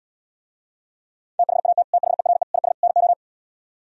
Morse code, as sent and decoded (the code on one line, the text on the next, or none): - .... .- - / - .... .-. / .. ... / -. ---
THAT THERE IS NO